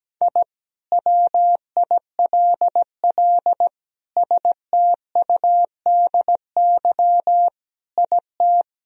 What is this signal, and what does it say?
Morse code, heard as a keyed tone, about 17 wpm, 700 Hz. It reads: I WILL STUDY IT
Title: Sentences from Top 1000 Words - Review Part 2 - 17wpm